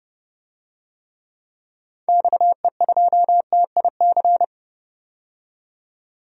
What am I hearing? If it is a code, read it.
Morse code, heard as a keyed tone, about 30 wpm, 700 Hz. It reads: XE2TIC
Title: DXCC-100 Call Signs - 6 Characters - 30wpm